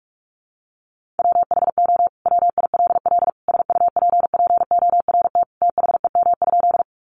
Morse code, heard as a keyed tone, about 44 wpm, 700 Hz. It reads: WHO WILL SUPPORT THEM?
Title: Sentences from Top 800 Words 45wpm